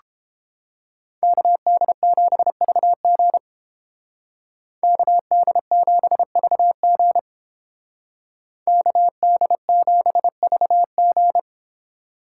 Morse code, heard as a keyed tone, about 28 wpm, 700 Hz. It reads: KD7VG KD7VG KD7VG